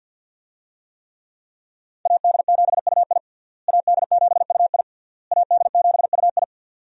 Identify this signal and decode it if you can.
Morse code, heard as a keyed tone, about 51 wpm, 700 Hz. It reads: AD7UI AD7UI AD7UI
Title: USA Call Signs - 5 Characters - Repeated 3x - 50wpm